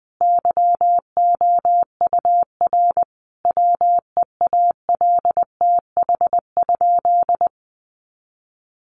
Morse code, heard as a keyed tone, about 20 wpm, 700 Hz. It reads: YOUR WEALTH?
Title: Sentences from Top 1200 Words 20wpm